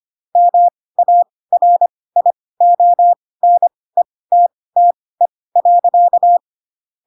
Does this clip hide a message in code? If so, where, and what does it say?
Yes: Morse code from 0.4 to 6.4 s, heard as a keyed tone, about 25 wpm, spaced as for 18 wpm, 700 Hz. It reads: MARIONETTE.